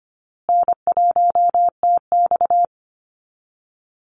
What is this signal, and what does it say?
Morse code, heard as a keyed tone, about 25 wpm, 700 Hz. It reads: N1TX
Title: Easy Call Signs Repeated 3x - 25wpm